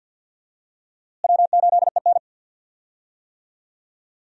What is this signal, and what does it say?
Morse code, heard as a keyed tone, about 53 wpm, 700 Hz. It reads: W8EN
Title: USA Call Signs - 4 Characters - Repeated 3x - 50wpm